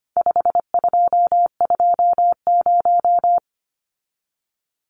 Morse code, heard as a keyed tone, about 25 wpm, 700 Hz. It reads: 5220
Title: Numbers - 4-Digit 25wpm